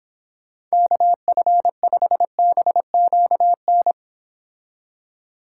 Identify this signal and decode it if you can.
Morse code, heard as a keyed tone, about 26 wpm, 700 Hz. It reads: KF5BQN